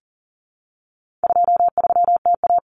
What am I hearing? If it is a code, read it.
Morse code, heard as a keyed tone, about 40 wpm, 700 Hz. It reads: 23TA
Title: Common Cut Numbers - 4-Digit 40wpm